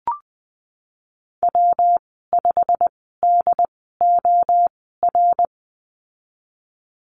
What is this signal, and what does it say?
Morse code, heard as a keyed tone, about 20 wpm, spaced as for 15 wpm, 700 Hz. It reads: W5DOR
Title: 147 Lesson 24g - Letter y - Call Signs 20wpm